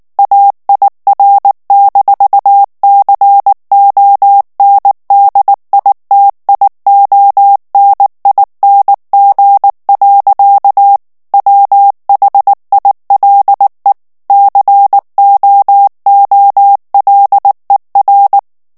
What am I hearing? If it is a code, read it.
Morse code, heard as a keyed tone, about 19 wpm, 800 Hz. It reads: AIR-CONDITIONING. WHILE COOLER
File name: morse mp3 / ANGLAIS texte clair 2 vitesse 18 mots minute